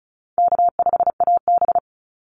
Morse code, heard as a keyed tone, about 34 wpm, 700 Hz. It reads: K5AB
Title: Easy Call Signs Repeated 3x - 35wpm